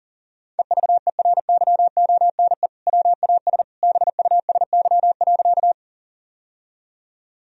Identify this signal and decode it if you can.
Morse code, heard as a keyed tone, about 40 wpm, 700 Hz. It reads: EVERYONE WAS BUSY.
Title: Sentences from Top 1000 Words - Review Part 3 - 40wpm